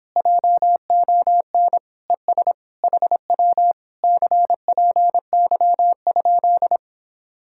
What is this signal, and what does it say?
Morse code, heard as a keyed tone, about 26 wpm, 700 Hz. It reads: JON ES HW CPY?